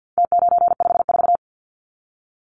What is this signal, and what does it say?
Morse code, heard as a keyed tone, about 50 wpm, 700 Hz. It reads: T954
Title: Numbers - Contest Style - 4-Digit 50wpm